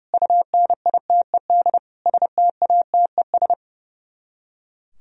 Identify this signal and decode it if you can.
Morse code, heard as a keyed tone, about 38 wpm, 700 Hz. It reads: UNITED STATES